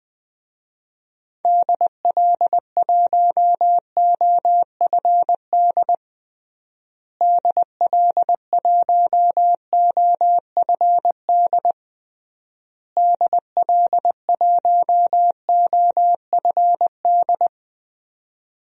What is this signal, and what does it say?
Morse code, heard as a keyed tone, about 20 wpm, 700 Hz. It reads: DL1OFD DL1OFD DL1OFD
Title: DXCC-100 Call Signs - 6 Characters - Repeated 3x - 20wpm